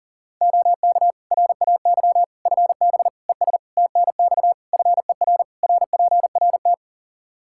Morse code, heard as a keyed tone, about 40 wpm, 700 Hz. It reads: OK RAY FB ES TNX FER RPRT
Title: Practice QSO - Exchange 2 - Info - 40wpm